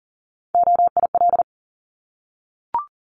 Morse code, heard as a keyed tone, about 42 wpm, 690 Hz. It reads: OIL
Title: Multi-letter Phonemes 40wpm